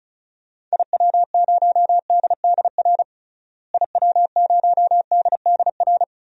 Morse code, heard as a keyed tone, about 36 wpm, 700 Hz. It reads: IW0DDR IW0DDR
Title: DXCC-100 Call Signs - 6 Characters - Repeated 3x - 35wpm